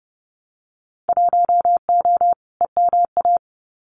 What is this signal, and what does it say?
Morse code, heard as a keyed tone, about 30 wpm, 700 Hz. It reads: 1O EMA
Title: Winter Field Day Exchanges 30wpm